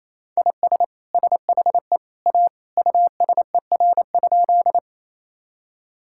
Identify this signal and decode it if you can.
Morse code, heard as a keyed tone, about 27 wpm, 700 Hz. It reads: IS SHE A USER?
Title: Sentences from Top 1000 Words - Review Part 5 - 28wpm